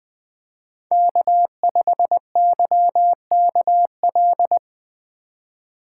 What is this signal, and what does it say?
Morse code, heard as a keyed tone, about 20 wpm, 700 Hz. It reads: K5YKL
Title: USA Call Signs - 3x Character Spacing - 20wpm